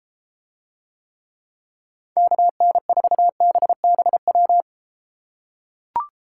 Morse code, heard as a keyed tone, about 32 wpm, 700 Hz. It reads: KN4BBW